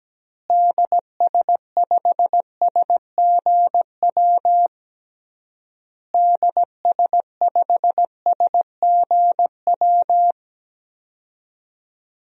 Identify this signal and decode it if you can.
Morse code, heard as a keyed tone, about 17 wpm, 700 Hz. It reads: DS5SGW DS5SGW
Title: DXCC-100 Call Signs - 6 Characters - Repeated 3x - 17wpm